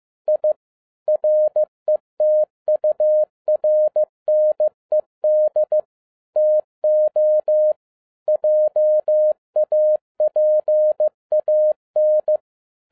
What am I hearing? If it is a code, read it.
Morse code, heard as a keyed tone, about 15 wpm, 610 Hz. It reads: I RETURNED TO JAPAN
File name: 2011-10-mp3 / n07-A DOT 15wpm.mp3